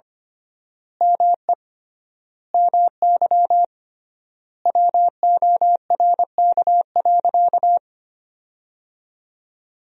Morse code, heard as a keyed tone, about 25 wpm, 700 Hz. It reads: ME MY WORK.
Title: 3x Word-Spacing - Sentences from Top 200 Words - 25wpm